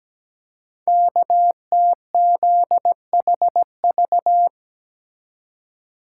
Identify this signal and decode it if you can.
Morse code, heard as a keyed tone, about 17 wpm, 700 Hz. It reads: KTZHV
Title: Code Groups - 5 Letters - 17wpm